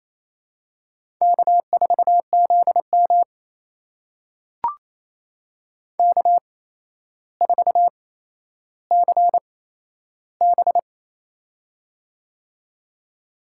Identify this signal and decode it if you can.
Morse code, heard as a keyed tone, about 27 wpm, spaced as for 12 wpm, 700 Hz. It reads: K4ZM K4CB
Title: USA Call Signs - 4 Characters - 8x Character Spacing - 28wpm